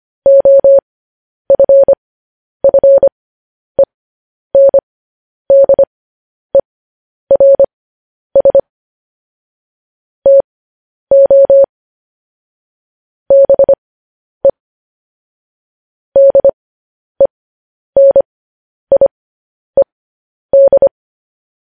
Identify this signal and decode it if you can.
Morse code, heard as a keyed tone, about 25 wpm, spaced as for 10 wpm, 550 Hz. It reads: OFFENDERS TO BE DENIED
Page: news headlines in morse code at 10 wpm